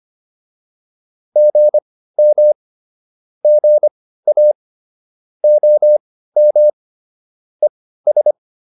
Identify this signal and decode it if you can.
Morse code, heard as a keyed tone, about 25 wpm, spaced as for 15 wpm, 600 Hz. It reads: GM GA OM ES